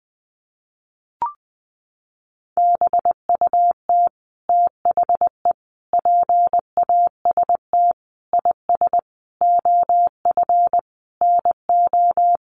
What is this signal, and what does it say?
Morse code, heard as a keyed tone, about 20 wpm, 700 Hz. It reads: BUT THE PAST IS OF NO
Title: Sentences from Top 1200 Words 20wpm